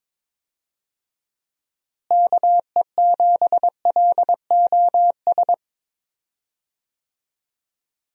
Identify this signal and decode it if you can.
Morse code, heard as a keyed tone, about 23 wpm, 700 Hz. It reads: KE7LOS
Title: USA Call Signs - 6 Characters - Repeated 3x - 22wpm